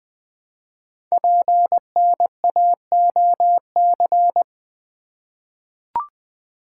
Morse code, heard as a keyed tone, about 20 wpm, 700 Hz. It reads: PNAOC
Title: Code Groups - 5 Letters - 20wpm